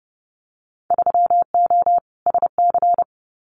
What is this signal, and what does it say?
Morse code, heard as a keyed tone, about 30 wpm, 700 Hz. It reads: 3O SC
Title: Winter Field Day Exchanges Encore 30wpm